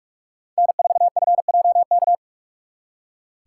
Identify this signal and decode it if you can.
Morse code, heard as a keyed tone, about 58 wpm, 700 Hz. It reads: N4FJK